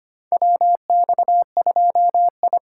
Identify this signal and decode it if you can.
Morse code, heard as a keyed tone, about 25 wpm, 700 Hz. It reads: WX2I
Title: USA Call Signs - 4 Characters - 3x Character Spacing - 25wpm